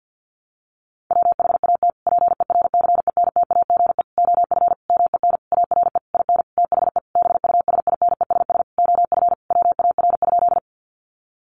Morse code, heard as a keyed tone, about 50 wpm, 700 Hz. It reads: WHAT PERCENTAGE OF MEN ARE IN THE BUSINESS OF WAR?
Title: Sentences from Top 1300 Words 50wpm